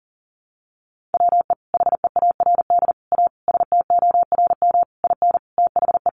Morse code, heard as a keyed tone, about 40 wpm, 700 Hz. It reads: WE HEARD A STORM IN THE
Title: Sentences from Top 1500 Words 40wpm